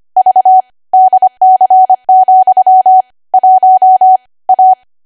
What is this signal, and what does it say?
Morse code, heard as a keyed tone, about 25 wpm, 750 Hz. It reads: V DC, 1 A